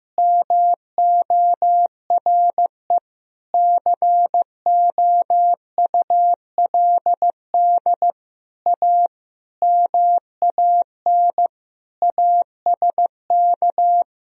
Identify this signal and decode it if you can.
Morse code, heard as a keyed tone, about 15 wpm, 700 Hz. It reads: MORE COULD A MAN ASK
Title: Sentences from Top 200 Words and Abbreviations 15wpm